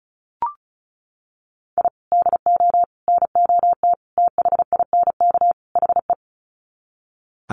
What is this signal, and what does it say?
Morse code, heard as a keyed tone, about 34 wpm, 700 Hz. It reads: I DO NOT THINK HE
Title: Sets of 5 Words 35wpm